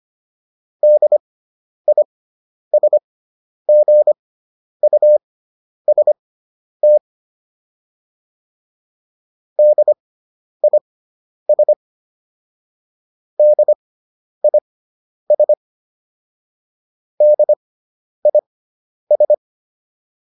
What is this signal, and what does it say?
Morse code, heard as a keyed tone, about 25 wpm, spaced as for 10 wpm, 600 Hz. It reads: DISGUST DIS DIS DIS